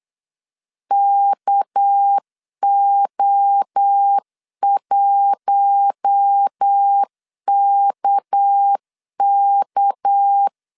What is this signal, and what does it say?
Morse code, heard as a keyed tone, about 8 wpm, 800 Hz. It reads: KO1KK
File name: LEZIONI / 6 PKX0 / 6 PK 8wpm 3003